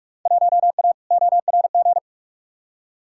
Morse code, heard as a keyed tone, about 46 wpm, 700 Hz. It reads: 1A ORG
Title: Field Day Exchanges Encore - 45wpm